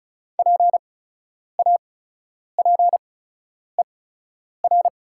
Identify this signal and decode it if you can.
Morse code, heard as a keyed tone, about 36 wpm, spaced as for 10 wpm, 700 Hz. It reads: PAPER
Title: 8x Character Spacing - Top 300 Words - 35wpm